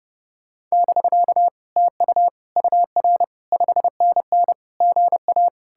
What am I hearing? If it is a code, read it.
Morse code, heard as a keyed tone, about 39 wpm, 700 Hz. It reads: <BK> TU UR 5NN GA